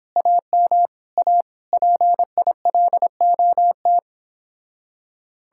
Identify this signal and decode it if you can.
Morse code, heard as a keyed tone, about 29 wpm, 700 Hz. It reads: AM A PILOT